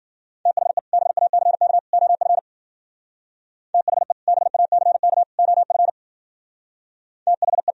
Morse code, heard as a keyed tone, about 50 wpm, 700 Hz. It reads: THE BACK OF THE BACK OF THE